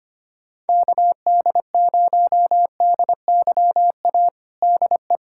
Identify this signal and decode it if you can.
Morse code, heard as a keyed tone, about 25 wpm, 700 Hz. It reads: KD0DYA DE
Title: USA Callsign DE Callsign 25wpm